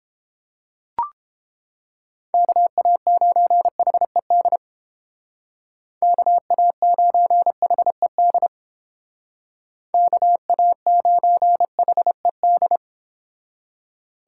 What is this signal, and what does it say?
Morse code, heard as a keyed tone, about 29 wpm, 700 Hz. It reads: KA9HED KA9HED KA9HED